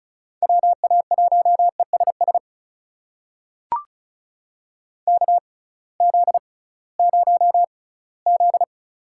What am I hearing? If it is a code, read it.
Morse code, heard as a keyed tone, about 36 wpm, spaced as for 21 wpm, 700 Hz. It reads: WA1ESS KZ0Z
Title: USA Call Signs - 6x Character Spacing - 35wpm